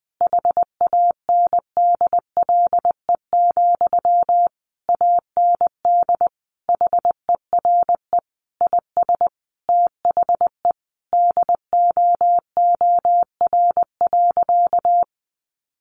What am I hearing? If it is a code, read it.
Morse code, heard as a keyed tone, about 20 wpm, 700 Hz. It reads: HANDLE, AND HERE IS THE DOOR.
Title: Sentences from Top 1100 Words 20wpm